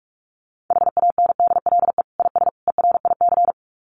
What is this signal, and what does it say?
Morse code, heard as a keyed tone, about 44 wpm, 700 Hz. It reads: HANDLE IS ERIC
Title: Common QSO Phrases 45wpm